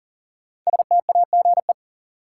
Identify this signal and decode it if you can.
Morse code, heard as a keyed tone, about 40 wpm, 700 Hz. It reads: STAGE
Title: Sentences from Top 1200 Words - New Word Drill Alt - 40wpm